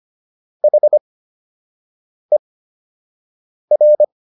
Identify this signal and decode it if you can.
Morse code, heard as a keyed tone, about 25 wpm, spaced as for 6 wpm, 600 Hz. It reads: HER